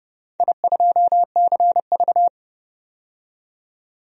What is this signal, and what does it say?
Morse code, heard as a keyed tone, about 31 wpm, 700 Hz. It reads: I2CV
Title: DXCC-100 Call Signs - 4 Characters - 30wpm